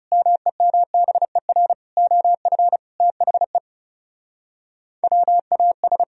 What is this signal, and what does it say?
Morse code, heard as a keyed tone, about 37 wpm, 700 Hz. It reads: MEMBER OF THE WAS
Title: Speed-Racing - Sets of 5 Words 30wpm